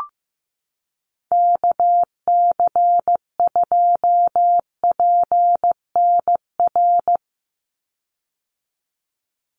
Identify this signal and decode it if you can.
Morse code, heard as a keyed tone, about 15 wpm, 700 Hz. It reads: KC2PNR
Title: USA Call Signs 15wpm